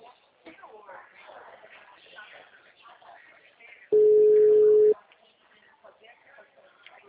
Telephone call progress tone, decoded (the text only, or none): ringback tone